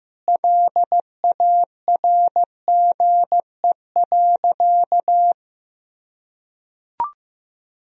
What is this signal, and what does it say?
Morse code, heard as a keyed tone, about 15 wpm, 700 Hz. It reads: LARGE.
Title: Sentences from Top 1000 Words - Review Part 1 - 15wpm